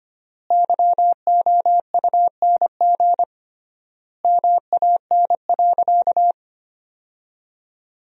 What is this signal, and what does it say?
Morse code, heard as a keyed tone, about 25 wpm, 700 Hz. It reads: YOUNG MAN.
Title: 3x Word-Spacing - Sentences from Top 400 Words - 25wpm